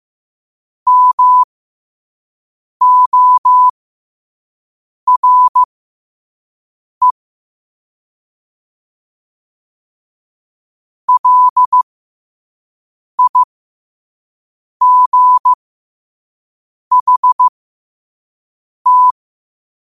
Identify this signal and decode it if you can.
Morse code, heard as a keyed tone, about 15 wpm, spaced as for 5 wpm, 1000 Hz. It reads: MORE LIGHT
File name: qotdfiles / 5wpm